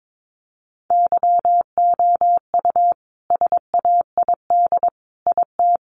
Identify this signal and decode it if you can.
Morse code, heard as a keyed tone, about 22 wpm, 700 Hz. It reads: YOU SAID IT
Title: Sentences from Top 1100 Words 22wpm